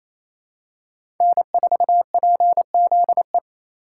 Morse code, heard as a keyed tone, about 28 wpm, 700 Hz. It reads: N4PZE
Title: USA Call Signs - 4x Character Spacing - 28wpm